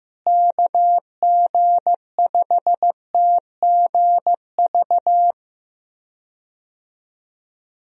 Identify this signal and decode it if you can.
Morse code, heard as a keyed tone, about 15 wpm, 700 Hz. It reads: KG5TGV
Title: USA Call Signs - 6 Characters - Repeated 3x - 15wpm